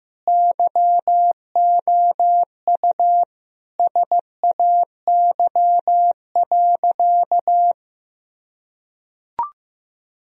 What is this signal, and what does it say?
Morse code, heard as a keyed tone, about 15 wpm, 700 Hz. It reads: YOU SAY.
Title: Sentences from Top 1000 Words - Review Part 1 - 15wpm